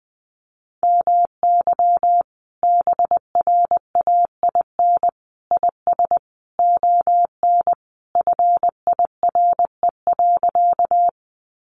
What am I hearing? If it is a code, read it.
Morse code, heard as a keyed tone, about 20 wpm, 700 Hz. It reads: MY BRAIN IS ON FIRE.
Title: Sentences from Top 1100 Words 20wpm